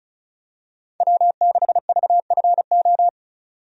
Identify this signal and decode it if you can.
Morse code, heard as a keyed tone, about 36 wpm, 700 Hz. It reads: W6VFO
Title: USA Call Signs - 5 Characters - 35wpm